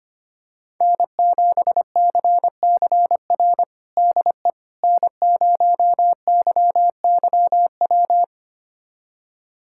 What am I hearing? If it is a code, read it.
Morse code, heard as a keyed tone, about 25 wpm, 700 Hz. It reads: N7CCR DE N0YYW
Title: USA Callsign DE Callsign - 5 Characters - 25wpm